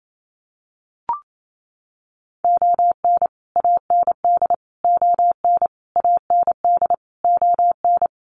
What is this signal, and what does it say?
Morse code, heard as a keyed tone, about 28 wpm, 700 Hz. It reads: ON AND ON AND ON
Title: Sets of 5 Words Encore2 28wpm